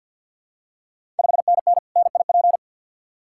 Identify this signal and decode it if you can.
Morse code, heard as a keyed tone, about 51 wpm, 690 Hz. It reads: 5NN NIP
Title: Ontario QSO Party Exchange 50wpm